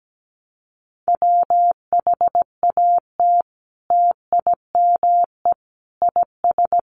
Morse code, heard as a keyed tone, about 17 wpm, 700 Hz. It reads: WHAT TIME IS
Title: Sentences from Top 100 Words 17wpm